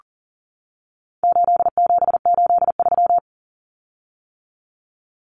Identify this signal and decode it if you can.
Morse code, heard as a keyed tone, about 40 wpm, 700 Hz. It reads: 8783
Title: Common Cut Numbers - 4-Digit 40wpm